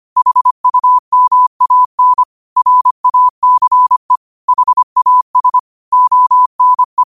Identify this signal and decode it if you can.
Morse code, heard as a keyed tone, about 25 wpm, 1000 Hz. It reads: HUMAN RACE HAS ONE
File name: qotdfiles / 25wpm